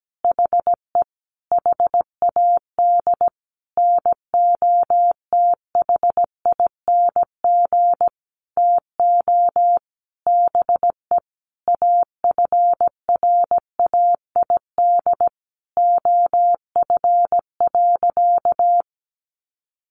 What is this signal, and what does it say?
Morse code, heard as a keyed tone, about 17 wpm, 700 Hz. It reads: HE HAD NOTHING TO BE AFRAID OF.
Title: Sentences from Top 1500 Words 17wpm